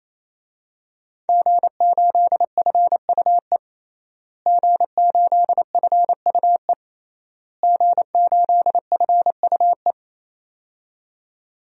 Morse code, heard as a keyed tone, about 27 wpm, 700 Hz. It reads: G8FUE G8FUE G8FUE
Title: DXCC-100 Call Signs - 5 Characters - Repeated 3x - 28wpm